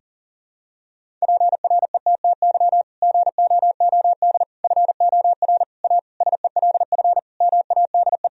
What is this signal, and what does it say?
Morse code, heard as a keyed tone, about 40 wpm, 700 Hz. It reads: PRETTY GOOD FOR A SELF MADE
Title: Sentences from Top 1000 Words - Review Part 1 - 40wpm